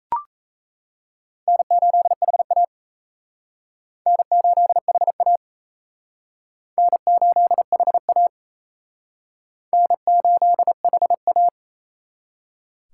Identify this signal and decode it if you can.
Morse code, heard as a keyed tone, about 35 wpm, 700 Hz. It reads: N8HA N8HA N8HA N8HA
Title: Speed Racing - USA 1x2 Call Signs 28wpm